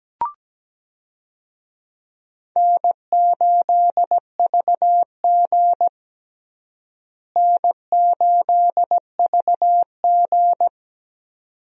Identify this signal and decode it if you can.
Morse code, heard as a keyed tone, about 17 wpm, 700 Hz. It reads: N8VG N8VG
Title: USA Call Signs - 4 Characters - Repeated 2x - 17wpm